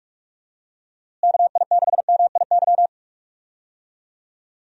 Morse code, heard as a keyed tone, about 46 wpm, 700 Hz. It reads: KI6MIY
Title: USA Call Signs - 6 Characters - Repeated 2x - 45wpm